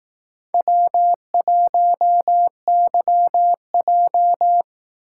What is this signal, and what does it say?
Morse code, heard as a keyed tone, about 18 wpm, 700 Hz. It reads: W1YJ
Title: Speed Racing - USA Call Signs 15wpm